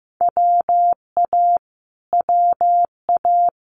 Morse code, heard as a keyed tone, about 15 wpm, 700 Hz. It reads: WA WA
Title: Multi-letter Phonemes Repeated 3x 15wpm